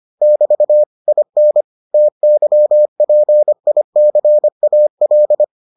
Morse code, heard as a keyed tone, about 25 wpm, 600 Hz. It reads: <BT> IN TYPICAL